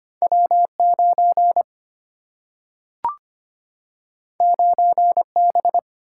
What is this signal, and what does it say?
Morse code, heard as a keyed tone, about 25 wpm, 700 Hz. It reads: W 9 9 B
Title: Single-Letter Single-Number 25wpm